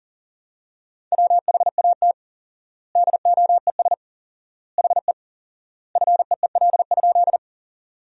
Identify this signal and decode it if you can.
Morse code, heard as a keyed tone, about 40 wpm, 700 Hz. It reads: WHAT DOES HE FEEL?
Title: 4x Word-Spacing - Sentences from Top 400 Words - 40wpm